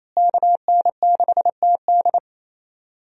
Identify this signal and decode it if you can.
Morse code, heard as a keyed tone, about 28 wpm, 700 Hz. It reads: KN6TD
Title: USA Call Signs - 5 Characters - Repeated 2x - 28wpm